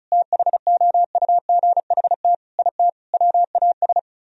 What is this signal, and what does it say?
Morse code, heard as a keyed tone, about 42 wpm, 700 Hz. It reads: THOUGHT IT WAS
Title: Speed-Racing - Sets of 4 Words 35wpm